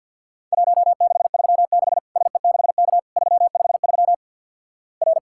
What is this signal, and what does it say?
Morse code, heard as a keyed tone, about 52 wpm, 700 Hz. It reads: 1636 SE6K 353 R
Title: Worked All Europe DX Contest - QTC Exchange - 50wpm